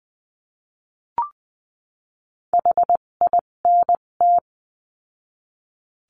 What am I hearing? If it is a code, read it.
Morse code, heard as a keyed tone, about 20 wpm, spaced as for 17 wpm, 700 Hz. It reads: HINT
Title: 257 Lesson 42b - Words 14-20wpm